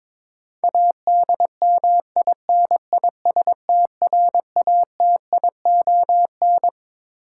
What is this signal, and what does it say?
Morse code, heard as a keyed tone, about 21 wpm, 700 Hz. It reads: ADMINISTRATION
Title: Sets of 5 Words - New Word Drill - 22wpm